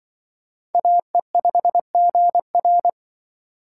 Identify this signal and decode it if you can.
Morse code, heard as a keyed tone, about 24 wpm, 700 Hz. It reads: AE5GR
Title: Speed Racing - USA 2x2 Call Signs 20wpm